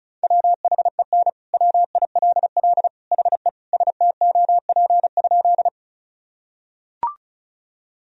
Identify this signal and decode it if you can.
Morse code, heard as a keyed tone, about 36 wpm, 700 Hz. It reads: WHEN WILL HE STOP?
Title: Sentences from Top 1000 Words - Review Part 1 - 35wpm